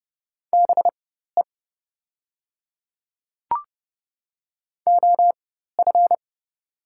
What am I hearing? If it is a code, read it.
Morse code, heard as a keyed tone, about 29 wpm, spaced as for 14 wpm, 700 Hz. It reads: BE OF
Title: 4x Character Spacing - Top 100 Words - 30wpm